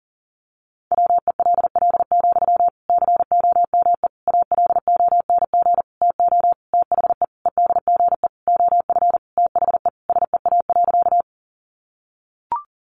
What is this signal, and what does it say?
Morse code, heard as a keyed tone, about 40 wpm, 700 Hz. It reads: WELL, COME ALONG TO THE EDGE OF THE SEA.
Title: Sentences from Top 1300 Words 40wpm